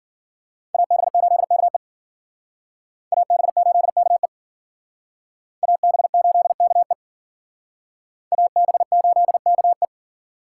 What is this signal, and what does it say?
Morse code, heard as a keyed tone, about 53 wpm, 700 Hz. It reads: AB8KE AB8KE AB8KE AB8KE